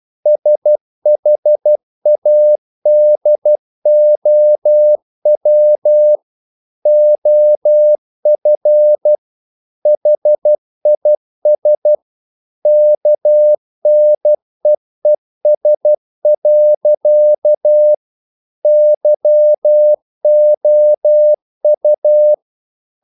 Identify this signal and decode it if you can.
Morse code, heard as a keyed tone, about 12 wpm, 600 Hz. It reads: SHADOW OF HIS KNEES. YOU
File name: mp3 files / the gardener - 12wpm